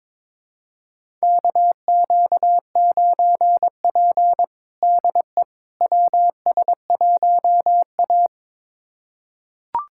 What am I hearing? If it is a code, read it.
Morse code, heard as a keyed tone, about 22 wpm, 700 Hz. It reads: KQ9P DE WS1A
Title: USA Callsign DE Callsign - 4 Characters - 22wpm